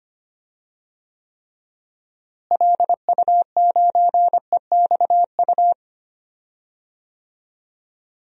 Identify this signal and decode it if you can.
Morse code, heard as a keyed tone, about 25 wpm, 700 Hz. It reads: LU9EXU
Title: DXCC-100 Call Signs - 6 Characters - Repeated 3x - 25wpm